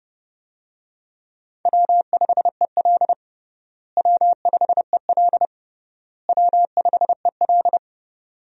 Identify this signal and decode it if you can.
Morse code, heard as a keyed tone, about 32 wpm, 700 Hz. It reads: W5EL W5EL W5EL
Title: USA Call Signs - 4 Characters - Repeated 3x - 30wpm